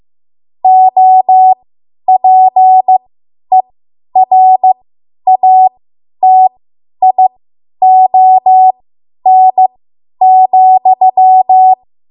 Morse code, heard as a keyed tone, about 15 wpm, spaced as for 10 wpm, 750 Hz. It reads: OPERATION,
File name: Morse / 10WPM